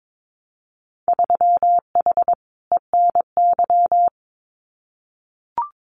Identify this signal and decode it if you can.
Morse code, heard as a keyed tone, about 22 wpm, 700 Hz. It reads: 3H ENY
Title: Winter Field Day Exchanges Encore 22wpm